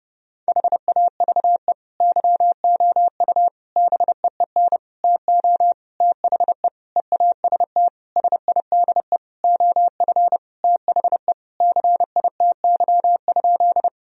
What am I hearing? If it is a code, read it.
Morse code, heard as a keyed tone, about 30 wpm, 700 Hz. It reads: HAVE YOU BEEN TO THE EAST SIDE OF THE CITY?
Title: Sentences from Top 1000 Words - Review Part 3 - 30wpm